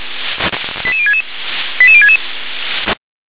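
Five-tone selective call (ZVEI), heard as quick runs of five tones, tones of about 80 ms. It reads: 88066 88066
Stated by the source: ZVEI five-tone selective call received on SDR.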